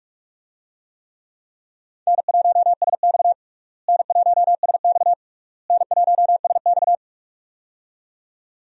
Morse code, heard as a keyed tone, about 46 wpm, 700 Hz. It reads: N1SX N1SX N1SX
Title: USA Call Signs - 4 Characters - Repeated 3x - 45wpm